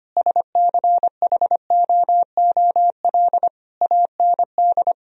Morse code, heard as a keyed tone, about 25 wpm, 700 Hz. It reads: SCHOOL AND
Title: Sentences from Top 1000 Words - Review Part 4 - 25wpm